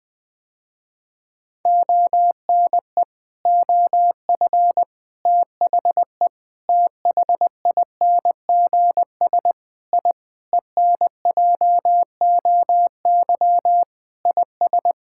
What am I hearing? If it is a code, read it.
Morse code, heard as a keyed tone, about 20 wpm, 700 Hz. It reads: ONE OF THE THINGS I ENJOY IS
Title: Sentences with 5-Word Phrases - Spoken Beginning - 20wpm